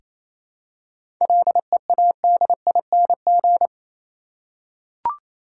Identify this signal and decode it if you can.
Morse code, heard as a keyed tone, about 28 wpm, 700 Hz. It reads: LEADING
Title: Seven Letter Words - 28wpm